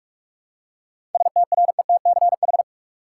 Morse code, heard as a keyed tone, about 46 wpm, 700 Hz. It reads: STRETCH
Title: Seven Letter Words - 45wpm